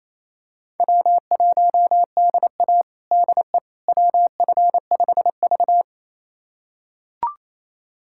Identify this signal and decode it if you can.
Morse code, heard as a keyed tone, about 28 wpm, 700 Hz. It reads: W1DA DE WF5V
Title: USA Callsign DE Callsign - 4 Characters - 28wpm